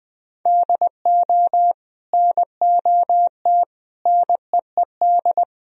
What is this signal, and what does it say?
Morse code, heard as a keyed tone, about 23 wpm, 700 Hz. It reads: DO NOT NEED
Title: Speed-Racing - Sets of 4 Words 20wpm